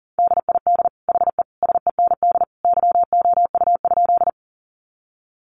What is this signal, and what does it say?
Morse code, heard as a keyed tone, about 40 wpm, 700 Hz. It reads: DID HE SEND YOU?
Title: Sentences from Top 1500 Words 40wpm